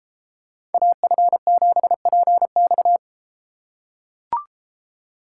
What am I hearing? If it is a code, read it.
Morse code, heard as a keyed tone, about 31 wpm, 700 Hz. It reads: AF7PX